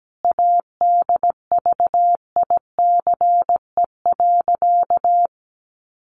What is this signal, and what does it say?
Morse code, heard as a keyed tone, about 17 wpm, 700 Hz. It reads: ADVICE.
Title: Sentences from Top 1100 Words 17wpm